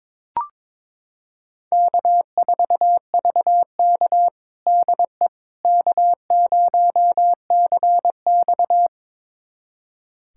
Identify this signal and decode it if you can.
Morse code, heard as a keyed tone, about 22 wpm, 700 Hz. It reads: K4VK DE K0CX
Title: USA Callsign DE Callsign - 4 Characters - 22wpm